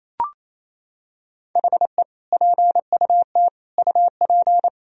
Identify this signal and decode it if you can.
Morse code, heard as a keyed tone, about 27 wpm, 700 Hz. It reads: HE PUT UP
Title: Sentences from Top 1000 Words - Review Part 4 - 28wpm